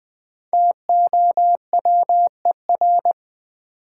Morse code, heard as a keyed tone, about 20 wpm, 700 Hz. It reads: TOWER
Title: Five Letter Words - 20wpm